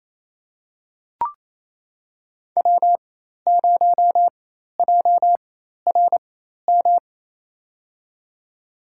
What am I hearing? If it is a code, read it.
Morse code, heard as a keyed tone, about 28 wpm, spaced as for 13 wpm, 700 Hz. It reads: W0JRM
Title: USA Call Signs - 5 Characters - 4x Character Spacing - 28wpm